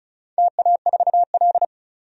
Morse code, heard as a keyed tone, about 34 wpm, 700 Hz. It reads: TA4L
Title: DXCC-100 Call Signs - 4 Characters - 35wpm